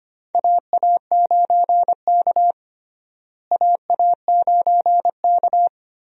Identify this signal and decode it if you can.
Morse code, heard as a keyed tone, about 25 wpm, 700 Hz. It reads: AA9K AA9K
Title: USA Call Signs - 4 Characters - Repeated 2x - 25wpm